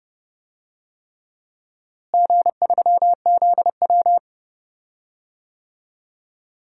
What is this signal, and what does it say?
Morse code, heard as a keyed tone, about 31 wpm, 700 Hz. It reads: G3ZW
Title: DXCC-100 Call Signs - 4 Characters - Repeated 3x - 30wpm